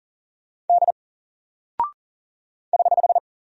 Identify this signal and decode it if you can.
Morse code, heard as a keyed tone, about 40 wpm, 700 Hz. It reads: D <HH>
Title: Single Character (Letter:Number:Punctuation:Prosign) 40wpm